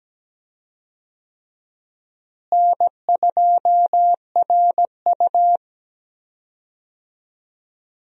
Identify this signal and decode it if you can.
Morse code, heard as a keyed tone, about 17 wpm, 700 Hz. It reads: N2RU
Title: USA Call Signs - 4 Characters - Repeated 3x - 17wpm